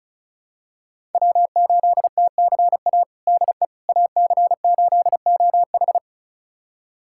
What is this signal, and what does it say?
Morse code, heard as a keyed tone, about 36 wpm, 700 Hz. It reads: W8TCA DE AC8OH
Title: USA Callsign DE Callsign - 5 Characters - 35wpm